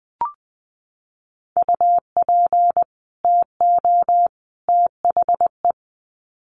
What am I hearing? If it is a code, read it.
Morse code, heard as a keyed tone, about 20 wpm, 700 Hz. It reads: UP TO THE
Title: Sets of 3 Words 20wpm